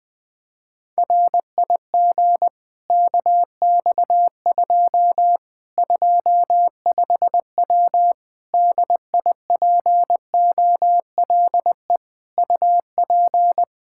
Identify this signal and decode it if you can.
Morse code, heard as a keyed tone, about 21 wpm, 700 Hz. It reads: RIG KX2 25W DIPOLE UP
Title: Practice QSO - Exchange 2 - Info - 20wpm